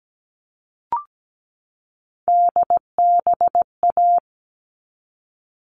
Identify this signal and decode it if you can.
Morse code, heard as a keyed tone, about 17 wpm, 700 Hz. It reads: DBA
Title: All Cut Numbers - 3-Digit 17wpm